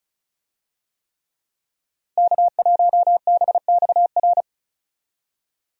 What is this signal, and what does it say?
Morse code, heard as a keyed tone, about 34 wpm, 700 Hz. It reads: K1BXR
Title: USA Call Signs - 5 Characters - Repeated 3x - 35wpm